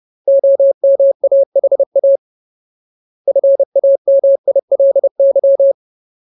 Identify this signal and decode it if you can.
Morse code, heard as a keyed tone, about 30 wpm, 550 Hz. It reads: OMAHA FAMILY
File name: Ultimate Word List / 30WPM Ultimate